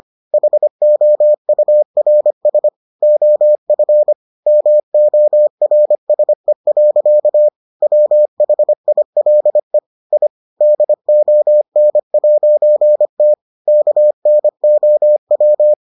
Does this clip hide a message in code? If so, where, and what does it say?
Yes: Morse code from 0.3 to 15.7 s, heard as a keyed tone, about 25 wpm, 600 Hz. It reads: HOURS OF MORSE. WHILE I DON'T KNOW